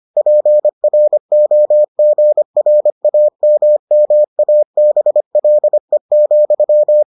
Morse code, heard as a keyed tone, about 25 wpm, 600 Hz. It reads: PROGRAMMABLE,